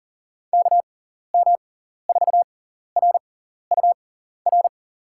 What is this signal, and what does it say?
Morse code, heard as a keyed tone, about 39 wpm, spaced as for 14 wpm, 700 Hz. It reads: KM4RUR
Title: USA Call Signs - 6 Characters - 6x Character Spacing - 40wpm